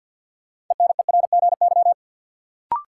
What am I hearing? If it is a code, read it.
Morse code, heard as a keyed tone, about 52 wpm, 700 Hz. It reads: ENERGY
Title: Sentences from Top 900 Words - New Word Drill - 50wpm